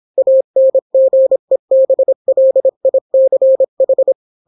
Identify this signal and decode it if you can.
Morse code, heard as a keyed tone, about 26 wpm, 530 Hz. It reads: ANGEBLICH